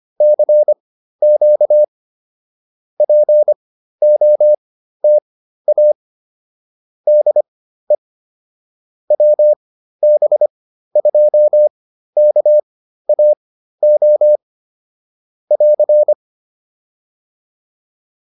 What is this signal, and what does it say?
Morse code, heard as a keyed tone, about 25 wpm, spaced as for 13 wpm, 600 Hz. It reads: CQ POTA DE WB2KAO <AR>